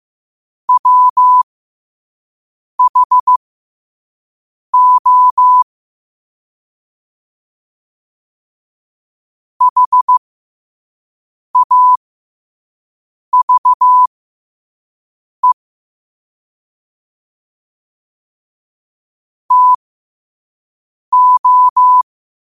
Morse code, heard as a keyed tone, about 15 wpm, spaced as for 5 wpm, 1000 Hz. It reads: WHO HAVE TO
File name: qotdfiles / 5wpm